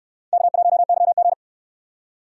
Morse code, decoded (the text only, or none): K1YG